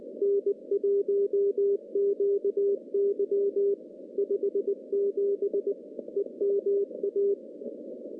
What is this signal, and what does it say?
Morse code, heard as a keyed tone, about 19 wpm, 400 Hz. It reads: N1QY 57 EMA
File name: contests / CW-SS-k5tr- -0100